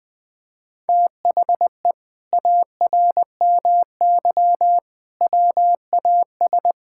Morse code, heard as a keyed tone, about 20 wpm, 700 Hz. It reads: THE ARMY WAS
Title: Sentences from Top 1000 Words - Review Part 5 - 20wpm